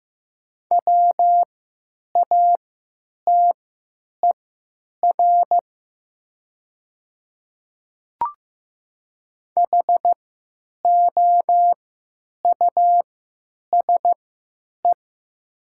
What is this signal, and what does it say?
Morse code, heard as a keyed tone, about 15 wpm, spaced as for 9 wpm, 700 Hz. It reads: WATER HOUSE
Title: 3x Character Spacing - Top 300 Words - 15wpm